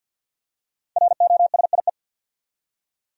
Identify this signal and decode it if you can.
Morse code, heard as a keyed tone, about 51 wpm, 700 Hz. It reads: ROSIE
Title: USA Common Names - 50wpm